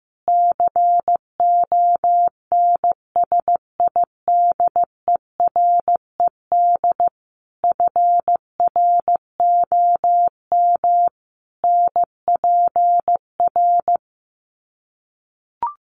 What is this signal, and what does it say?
Morse code, heard as a keyed tone, about 15 wpm, 700 Hz. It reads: CONSIDERED FROM NPR
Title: Sets of 5 Words Encore 15wpm